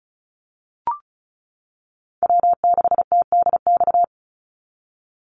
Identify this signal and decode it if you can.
Morse code, heard as a keyed tone, about 34 wpm, 700 Hz. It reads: W6TDX
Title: NA SOTA Chaser Callsigns 35wpm